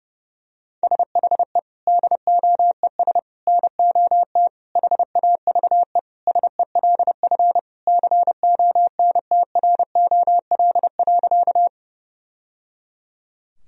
Morse code, heard as a keyed tone, about 32 wpm, 700 Hz. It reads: SHE DOES NOT HAVE SELF CONTROL.
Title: Sentences from Top 1000 Words - Review Part 3 - 30wpm